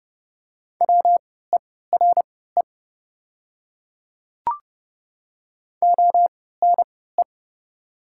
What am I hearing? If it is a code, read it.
Morse code, heard as a keyed tone, about 30 wpm, spaced as for 17 wpm, 700 Hz. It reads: WERE ONE
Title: 3x Character Spacing - Top 100 Words - 30wpm